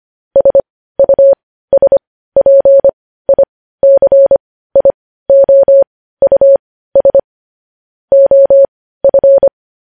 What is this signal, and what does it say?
Morse code, heard as a keyed tone, about 25 wpm, spaced as for 15 wpm, 550 Hz. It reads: SUSPICIOUS OF
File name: podcast / News headlines-1x-25x15